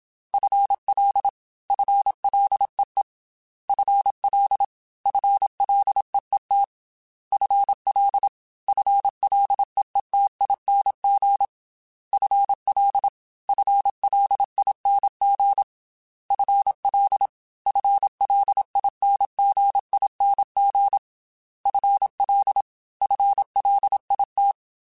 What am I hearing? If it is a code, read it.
Morse code, heard as a keyed tone, about 27 wpm, 800 Hz. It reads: FL FLEE FL FLEET FL FLEETING FL FLING FL FLINGING FL FLIT
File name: FL / FL mp3s / FL Prefixes 1x1 @25